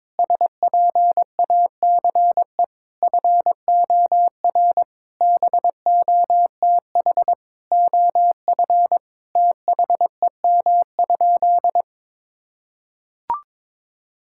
Morse code, heard as a keyed tone, about 22 wpm, 700 Hz. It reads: SPACE FOR BOTH OF THEM?
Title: Sentences from Top 1000 Words - Review Part 5 - 22wpm